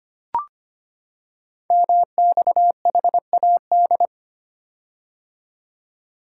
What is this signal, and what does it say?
Morse code, heard as a keyed tone, about 25 wpm, 700 Hz. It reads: MXHAD
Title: Code Groups - 5 Letters - 25wpm